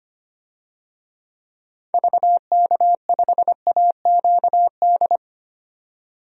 Morse code, heard as a keyed tone, about 25 wpm, 700 Hz. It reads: VK5AQD
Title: DXCC-100 Call Signs - 6 Characters - 25wpm